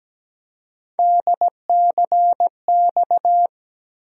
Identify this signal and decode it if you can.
Morse code, heard as a keyed tone, about 17 wpm, 700 Hz. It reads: DCX
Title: USA Call Sign Postfixes - 3 Letters 17wpm